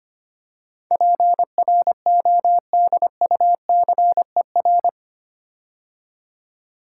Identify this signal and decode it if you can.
Morse code, heard as a keyed tone, about 26 wpm, 700 Hz. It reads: PRODUCER
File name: 8 Letter Words - First 3 Letters Given - 25wpm